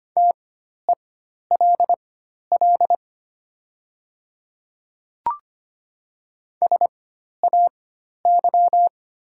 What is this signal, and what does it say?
Morse code, heard as a keyed tone, about 25 wpm, spaced as for 12 wpm, 700 Hz. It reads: TELL SAY
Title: 4x Character Spacing - Top 200 Words - 25wpm